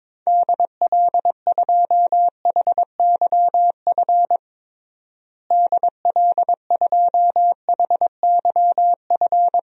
Morse code, heard as a keyed tone, about 22 wpm, 700 Hz. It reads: DL2HYF DL2HYF
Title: DXCC-100 Call Signs - 6 Characters - Repeated 2x - 22wpm